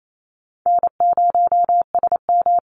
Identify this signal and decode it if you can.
Morse code, heard as a keyed tone, about 28 wpm, 700 Hz. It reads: N0SM
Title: Call Signs - 28wpm